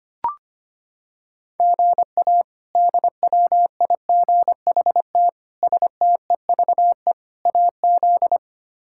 Morse code, heard as a keyed tone, about 25 wpm, 700 Hz. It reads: GA DWIGHT STEVE AZ
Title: K1USN SST QSO Practice - 25wpm